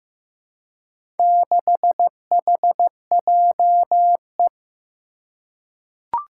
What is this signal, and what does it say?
Morse code, heard as a keyed tone, about 15 wpm, 700 Hz. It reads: 6HJE
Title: Four Challenging Letter-Number Combinations 15wpm